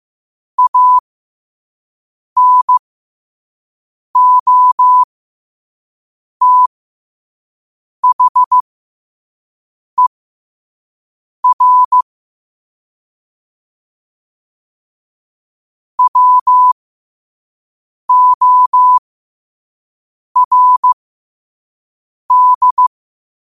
Morse code, heard as a keyed tone, about 15 wpm, spaced as for 5 wpm, 1000 Hz. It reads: ANOTHER WORD
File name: qotdfiles / 5wpm